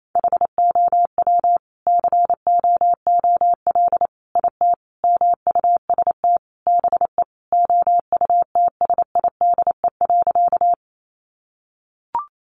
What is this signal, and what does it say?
Morse code, heard as a keyed tone, about 28 wpm, 700 Hz. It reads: HOW COOL IT MUST BE OUTSIDE.
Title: Sentences from Top 1200 Words 28wpm